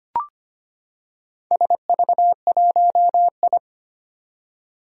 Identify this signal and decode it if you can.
Morse code, heard as a keyed tone, about 25 wpm, 700 Hz. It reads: SV1I
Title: Four Challenging Letter-Number Combinations 25wpm